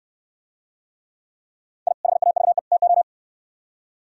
Morse code, heard as a keyed tone, about 65 wpm, 700 Hz. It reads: I HAVE TO